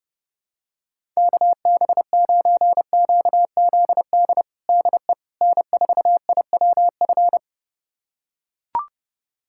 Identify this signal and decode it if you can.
Morse code, heard as a keyed tone, about 30 wpm, 700 Hz. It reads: KB9QZD DE N4IWF
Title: USA Callsign DE Callsign 30wpm